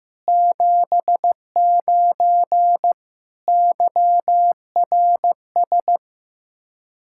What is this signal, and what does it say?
Morse code, heard as a keyed tone, about 15 wpm, 700 Hz. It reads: 79 YRS
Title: K1USN SST QSO Practice 15wpm